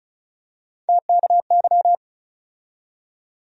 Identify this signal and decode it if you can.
Morse code, heard as a keyed tone, about 36 wpm, 700 Hz. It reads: TKY
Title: USA Call Sign Postfixes - 3 Letters 35wpm